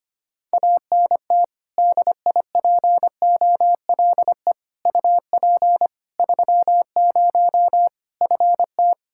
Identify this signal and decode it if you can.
Morse code, heard as a keyed tone, about 25 wpm, 700 Hz. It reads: ANT DIPOLE UP 30 FT
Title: K1USN SST QSO Practice 25wpm